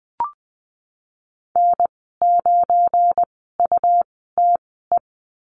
Morse code, heard as a keyed tone, about 19 wpm, spaced as for 14 wpm, 700 Hz. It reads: N9UTE
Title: 173 Lesson 28g - Number 9 - Call Signs 20wpm